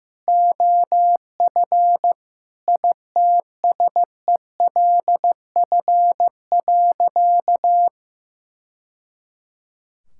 Morse code, heard as a keyed tone, about 16 wpm, 690 Hz. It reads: OF ITSELF.
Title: Sentences from Top 1000 Words - Review Part 5 - 15wpm